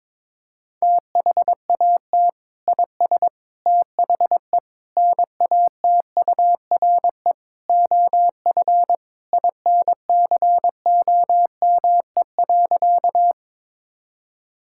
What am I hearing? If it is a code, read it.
Morse code, heard as a keyed tone, about 22 wpm, 700 Hz. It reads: THAT IS THE NATURE OF INCOME.
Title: Sentences from Top 1000 Words - Review Part 5 - 22wpm